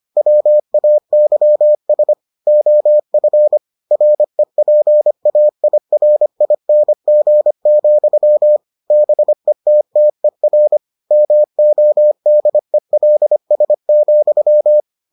Morse code, heard as a keyed tone, about 25 wpm, 600 Hz. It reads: WAYS OF REPAIRING, BETTER MODELS,